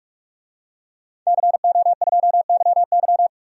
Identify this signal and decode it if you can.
Morse code, heard as a keyed tone, about 46 wpm, 700 Hz. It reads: CO2YY
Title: DXCC-100 Call Signs - 5 Characters - Repeated 3x - 45wpm